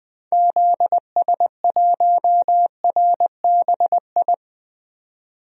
Morse code, heard as a keyed tone, about 20 wpm, 700 Hz. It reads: ZS1RBI
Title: DXCC-100 Call Signs - 6 Characters - 20wpm